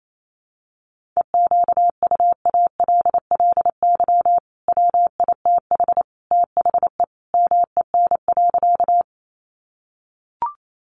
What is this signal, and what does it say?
Morse code, heard as a keyed tone, about 28 wpm, 700 Hz. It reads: EQUALLY WITH THE MEN.
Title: Sentences from Top 1400 Words 28wpm